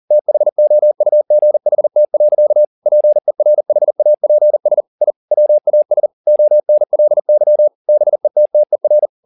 Morse code, heard as a keyed tone, about 40 wpm, 600 Hz. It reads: THOUGHT. PERHAPS I WAS ONLY BETTER
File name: the old man and the sea 2 40wpm